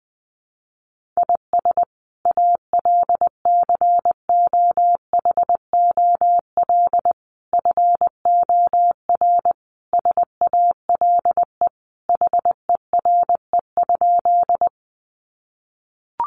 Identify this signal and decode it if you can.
Morse code, heard as a keyed tone, about 20 wpm, 700 Hz. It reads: IS ALCOHOL FOR SALE HERE?
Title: Sentences from Top 1400 Words 20wpm